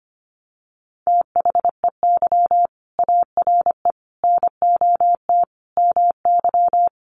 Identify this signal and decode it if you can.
Morse code, heard as a keyed tone, about 25 wpm, 700 Hz. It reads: THEY ARE NOT MY
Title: Sentences from Top 1100 Words 25wpm